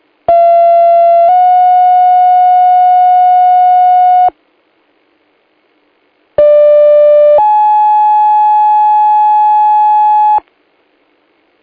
Two-tone page, heard as a one-second tone, then a three-second tone, about 680 then 730 Hz, about 590 then 850 Hz. It is heard from 0.3 to 10.4 s.